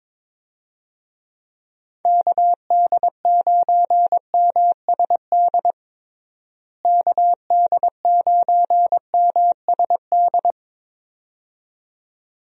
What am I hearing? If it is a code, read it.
Morse code, heard as a keyed tone, about 22 wpm, 700 Hz. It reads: KD9MSD KD9MSD
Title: USA Call Signs - 6 Characters - Repeated 2x - 22wpm